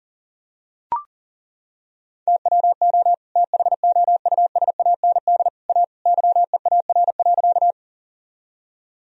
Morse code, heard as a keyed tone, about 40 wpm, 700 Hz. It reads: TWO THOUSAND A YEAR.
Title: Sentences from Top 1000 Words - Review Part 3 - 40wpm